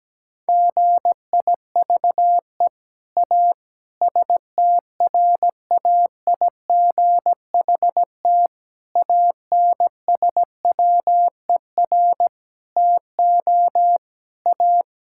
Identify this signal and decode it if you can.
Morse code, heard as a keyed tone, about 17 wpm, 700 Hz. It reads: GIVE A STRAIGHT ANSWER TO A
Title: Sentences from Top 1000 Words - Review Part 6 - 17wpm